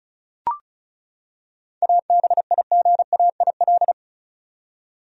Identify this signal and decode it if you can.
Morse code, heard as a keyed tone, about 34 wpm, 700 Hz. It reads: ABIGAIL
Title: USA Common Names - 35wpm